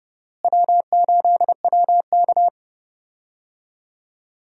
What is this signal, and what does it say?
Morse code, heard as a keyed tone, about 30 wpm, 700 Hz. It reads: W8WK
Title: USA Call Signs - 4 Characters - Repeated 2x - 30wpm